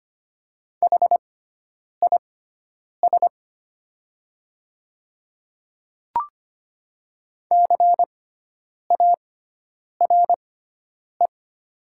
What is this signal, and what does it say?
Morse code, heard as a keyed tone, about 25 wpm, spaced as for 9 wpm, 690 Hz. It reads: HIS CARE
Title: 6x Character Spacing - Top 300 Words - 25wpm